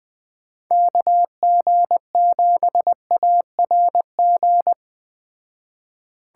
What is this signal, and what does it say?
Morse code, heard as a keyed tone, about 19 wpm, 700 Hz. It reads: KG7ARG
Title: Speed Racing - USA 2x3 Call Signs 17wpm